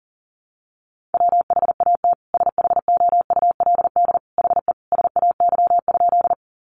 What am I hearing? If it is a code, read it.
Morse code, heard as a keyed tone, about 40 wpm, 700 Hz. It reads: WHAT SHOULD HE SAY?
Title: Sentences from Top 300 Words 40wpm